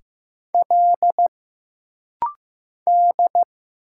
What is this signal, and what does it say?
Morse code, heard as a keyed tone, about 15 wpm, 700 Hz. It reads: L D
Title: Single Letter-Number Alt - Rapid-Fire 15wpm